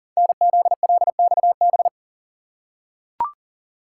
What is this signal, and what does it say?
Morse code, heard as a keyed tone, about 40 wpm, 700 Hz. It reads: NZLXB
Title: Code Groups - 5 Letters - 40wpm